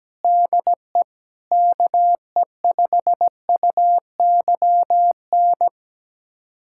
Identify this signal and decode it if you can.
Morse code, heard as a keyed tone, about 17 wpm, 700 Hz. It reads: DE KE5UYN
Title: USA Callsign DE Callsign 17wpm